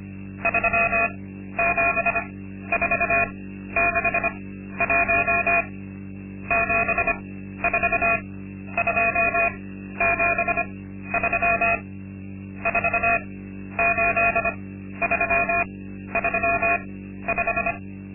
Morse code, heard as a keyed tone, about 25 wpm, spaced as for 12 wpm, 680 Hz. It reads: 37461 74273 48335
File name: Linksmakalnis-2